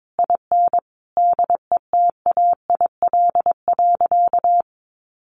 Morse code, heard as a keyed tone, about 22 wpm, 700 Hz. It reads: IN DETAIL.
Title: Sentences from Top 1500 Words 22wpm